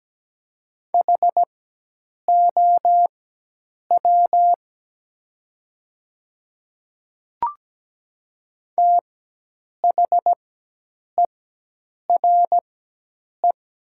Morse code, heard as a keyed tone, about 17 wpm, spaced as for 8 wpm, 700 Hz. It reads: HOW THERE
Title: 4x Character Spacing - Top 200 Words - 17wpm